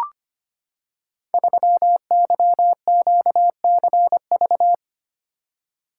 Morse code, heard as a keyed tone, about 25 wpm, 700 Hz. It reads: 3YQCV
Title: Code Groups - 5 Letter:Numbers - 25wpm